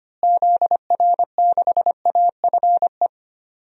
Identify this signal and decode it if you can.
Morse code, heard as a keyed tone, about 25 wpm, 700 Hz. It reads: ZR6AFE
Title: DXCC-100 Call Signs - 6 Characters - Repeated 2x - 25wpm